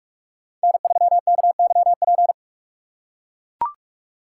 Morse code, heard as a keyed tone, about 44 wpm, 700 Hz. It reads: N3KYP